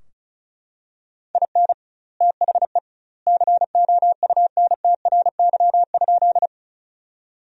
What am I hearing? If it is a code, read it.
Morse code, heard as a keyed tone, about 36 wpm, 700 Hz. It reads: IN THE COUNTRY?
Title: 2x Word-Spacing - Sentences from Top 300 Words - 35wpm